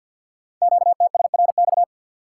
Morse code, heard as a keyed tone, about 50 wpm, 700 Hz. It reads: QTSRX
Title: Code Groups - 5 Letters - 50wpm